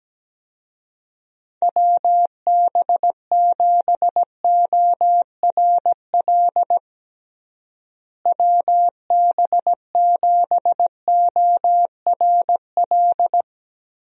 Morse code, heard as a keyed tone, about 17 wpm, 700 Hz. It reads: WB7ORL WB7ORL
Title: USA Call Signs - 6 Characters - Repeated 3x - 17wpm